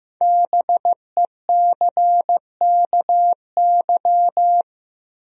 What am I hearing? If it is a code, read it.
Morse code, heard as a keyed tone, about 15 wpm, 700 Hz. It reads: BECKY
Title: USA Common Names - Max 5 Letters - 15wpm